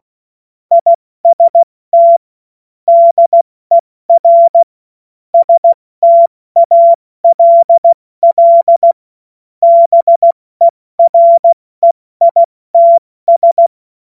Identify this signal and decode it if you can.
Morse code, heard as a keyed tone, about 16 wpm, 680 Hz. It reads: IST DER STALL BEREITS
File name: Witz-16-September-2025-70-F80